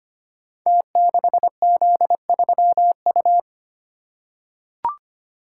Morse code, heard as a keyed tone, about 25 wpm, 700 Hz. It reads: T6Z3U
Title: Code Groups - 5 Letter:Numbers - 25wpm